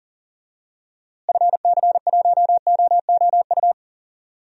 Morse code, heard as a keyed tone, about 40 wpm, 700 Hz. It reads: FC1OOU
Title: DXCC-100 Call Signs - 6 Characters - Repeated 3x - 40wpm